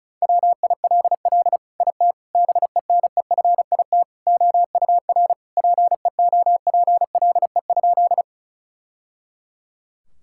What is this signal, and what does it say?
Morse code, heard as a keyed tone, about 36 wpm, 700 Hz. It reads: WILL IT BENEFIT OUR PEOPLE?
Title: Sentences from Top 1000 Words - Review Part 6 - 35wpm